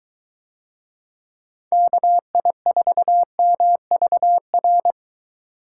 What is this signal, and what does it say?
Morse code, heard as a keyed tone, about 23 wpm, 700 Hz. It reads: KI4MVR